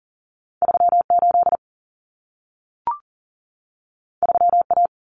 Morse code, heard as a keyed tone, about 40 wpm, 700 Hz. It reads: 38 3A
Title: Common Cut Numbers - 2-Digit 40wpm